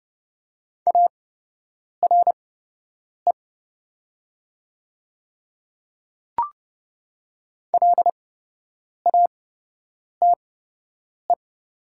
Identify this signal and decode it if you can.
Morse code, heard as a keyed tone, about 30 wpm, spaced as for 8 wpm, 700 Hz. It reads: ARE LATE
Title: 8x Character Spacing - Top 300 Words - 30wpm